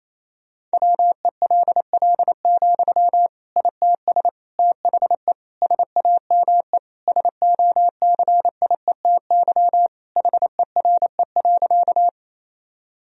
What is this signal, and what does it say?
Morse code, heard as a keyed tone, about 27 wpm, 700 Hz. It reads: WELL, ITS THE SAME SOCIETY HERE.
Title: Sentences from Top 1000 Words - Review Part 4 - 28wpm